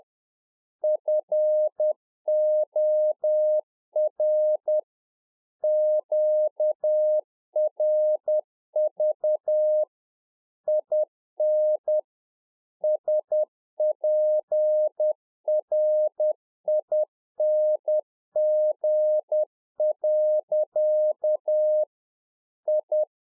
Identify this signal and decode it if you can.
Morse code, heard as a keyed tone, about 10 wpm, 610 Hz. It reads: FOR QRV IN SPRING. I